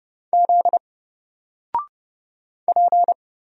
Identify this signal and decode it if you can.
Morse code, heard as a keyed tone, about 30 wpm, 700 Hz. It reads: Z P
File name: Single-Letter-Number-30wpm